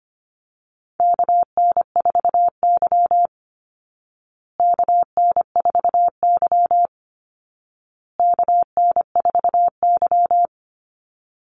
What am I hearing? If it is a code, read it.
Morse code, heard as a keyed tone, about 25 wpm, 700 Hz. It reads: KN4Y KN4Y KN4Y
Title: Call Signs Repeated 2x - 25wpm